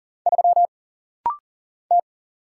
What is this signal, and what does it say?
Morse code, heard as a keyed tone, about 41 wpm, 700 Hz. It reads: 3 T
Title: Single Letter-Number - Rapid-Fire 40wpm